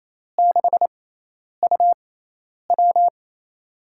Morse code, heard as a keyed tone, about 28 wpm, 700 Hz. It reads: 6 U W
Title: USA Call Signs - 4 Characters - 6x Character Spacing - 28wpm